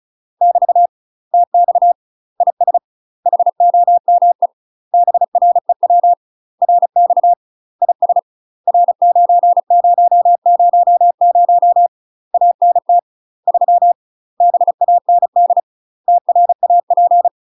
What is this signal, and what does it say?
Morse code, heard as a keyed tone, about 34 wpm, 700 Hz. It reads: <BT> TX IS HOME BREW RX IS R9000 ANT 3 BAND TRAP
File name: UK MASTER QSOs TRAINING / 35WPM / UK MASTER QSOs 0039